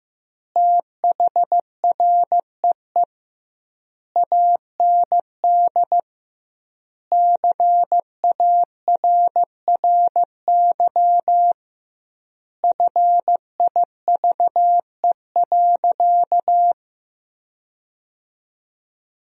Morse code, heard as a keyed tone, about 15 wpm, 700 Hz. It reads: THREE AND CARRY FIVE.
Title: 2x Word-Spacing - Sentences from Top 400 Words - 15wpm